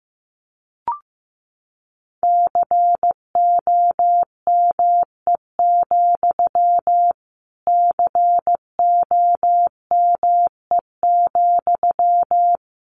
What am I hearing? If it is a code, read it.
Morse code, heard as a keyed tone, about 15 wpm, 700 Hz. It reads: COME, COME,
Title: Sentences from Top 1500 Words 15wpm